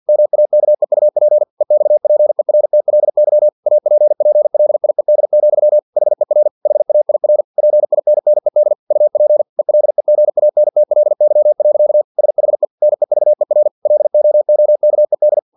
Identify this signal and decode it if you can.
Morse code, heard as a keyed tone, about 50 wpm, 600 Hz. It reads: MAKEUP EXPERTLY APPLIED, HER HAIR PINNED UP ELEGANTLY. SHE NEVER LOOKED
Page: Never 49wpm